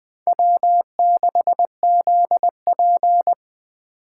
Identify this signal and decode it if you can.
Morse code, heard as a keyed tone, about 20 wpm, 700 Hz. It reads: W6ZP
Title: Speed Racing - USA Call Signs 17wpm